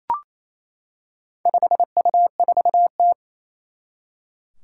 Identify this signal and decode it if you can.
Morse code, heard as a keyed tone, about 27 wpm, 700 Hz. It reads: 5U4T
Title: Four Challenging Letter-Number Combinations 28wpm